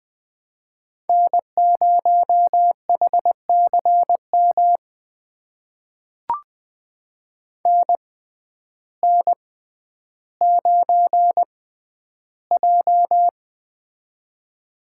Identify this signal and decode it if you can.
Morse code, heard as a keyed tone, about 20 wpm, spaced as for 11 wpm, 700 Hz. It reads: N0HCM NN9J
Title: USA Call Signs - 6x Character Spacing - 20wpm